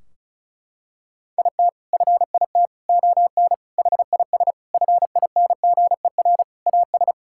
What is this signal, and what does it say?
Morse code, heard as a keyed tone, about 34 wpm, 700 Hz. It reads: IT FIT ON HIS FINGER AS
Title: Sentences from Top 1600 Words 35wpm